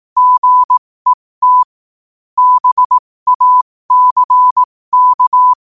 Morse code, heard as a keyed tone, about 18 wpm, spaced as for 15 wpm, 1000 Hz. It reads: GET BACK
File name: qotdfiles / 15wpm